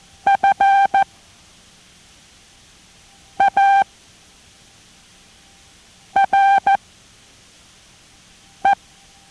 Morse code, heard as a keyed tone, about 14 wpm, spaced as for 4 wpm, 790 Hz. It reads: FARE